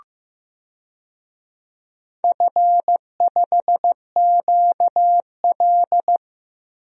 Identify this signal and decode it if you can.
Morse code, heard as a keyed tone, about 15 wpm, 700 Hz. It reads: F5QL
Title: DXCC-100 Call Signs - 4 Characters - Repeated 3x - 15wpm